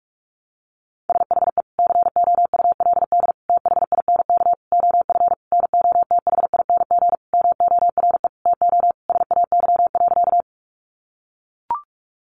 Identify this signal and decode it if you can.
Morse code, heard as a keyed tone, about 44 wpm, 700 Hz. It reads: SHE COULD THINK OF NOTHING MORE TO SAY.
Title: Sentences from Top 500 Words 45wpm